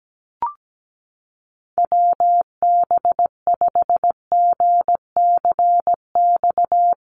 Morse code, heard as a keyed tone, about 17 wpm, 700 Hz. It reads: WB5GCX
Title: USA Call Signs Repeated 2x 17wpm